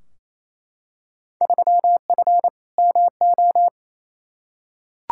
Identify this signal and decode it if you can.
Morse code, heard as a keyed tone, about 28 wpm, 700 Hz. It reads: 3F MO
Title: Field Day Exchanges Encore - 28wpm